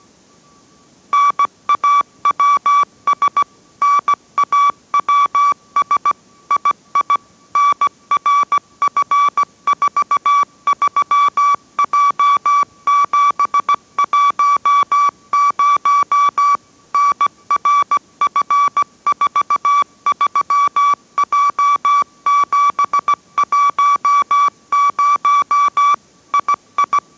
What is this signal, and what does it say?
Morse code, heard as a keyed tone, about 17 wpm, 1150 Hz. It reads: NAWS NAWS II NRF43J710 NRF43J710 II